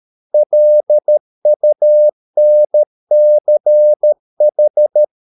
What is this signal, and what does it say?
Morse code, heard as a keyed tone, about 13 wpm, 600 Hz. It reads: LUNCH